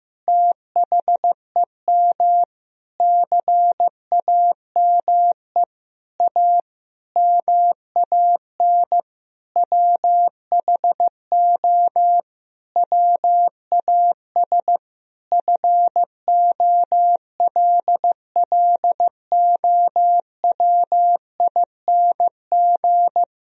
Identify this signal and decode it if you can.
Morse code, heard as a keyed tone, about 15 wpm, 700 Hz. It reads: THEM CAME A MAN WHO WAS FOLLOWING
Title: Sentences from Top 1000 Words - Review Part 4 - 15wpm